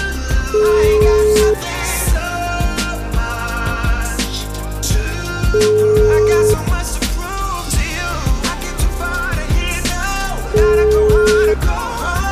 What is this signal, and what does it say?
Ringback tone (Europe, 425 Hz), with noise.